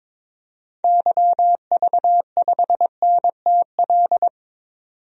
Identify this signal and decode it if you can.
Morse code, heard as a keyed tone, about 22 wpm, 700 Hz. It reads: YV5NTL
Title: DXCC-100 Call Signs - 6 Characters - 22wpm